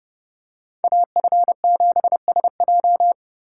Morse code, heard as a keyed tone, about 29 wpm, 700 Hz. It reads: AF7SJ